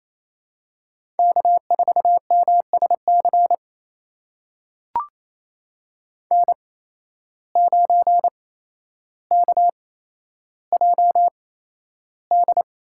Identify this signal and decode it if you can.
Morse code, heard as a keyed tone, about 28 wpm, spaced as for 12 wpm, 700 Hz. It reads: K4MSC N9KJD
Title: USA Call Signs - 5 Characters - 8x Character Spacing - 28wpm